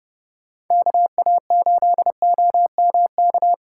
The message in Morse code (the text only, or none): KA8OMK